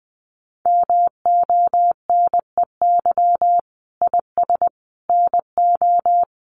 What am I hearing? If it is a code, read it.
Morse code, heard as a keyed tone, about 20 wpm, 700 Hz. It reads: MONEY IS NO
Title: Sentences from Top 500 Words 20wpm